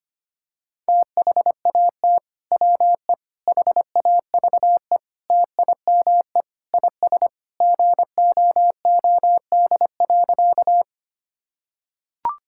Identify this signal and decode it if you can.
Morse code, heard as a keyed tone, about 25 wpm, 700 Hz. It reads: THAT WE HAVE TIME IS GOOD.
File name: practice / 25wpm / Sentences with 3-Word Phrases - 25wpm